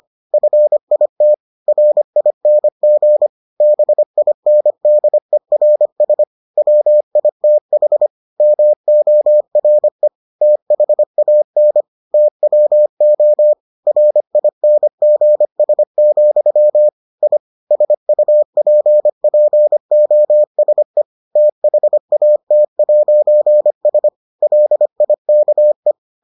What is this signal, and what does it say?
Morse code, heard as a keyed tone, about 25 wpm, 600 Hz. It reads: FIT RING BINDERS WITH MORE THAN TWO RINGS, I SUPPOSE THAT'S LIKE